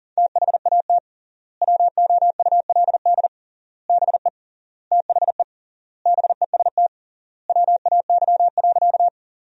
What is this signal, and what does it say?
Morse code, heard as a keyed tone, about 40 wpm, 700 Hz. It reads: THAT WOULD BE THE BEST WAY.
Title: 3x Word-Spacing - Sentences from Top 400 Words - 40wpm